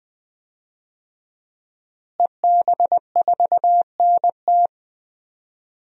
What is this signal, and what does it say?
Morse code, heard as a keyed tone, about 21 wpm, 700 Hz. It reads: EB4NT
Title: DXCC-100 Call Signs - 5 Characters - Repeated 2x - 20wpm